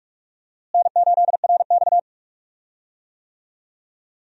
Morse code, heard as a keyed tone, about 42 wpm, 700 Hz. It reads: N8RX
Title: USA Call Signs - 4 Characters - Repeated 3x - 45wpm